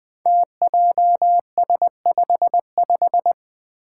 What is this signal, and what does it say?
Morse code, heard as a keyed tone, about 20 wpm, 700 Hz. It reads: TJS55
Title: Five Challenging Letter-Number Combinations 20wpm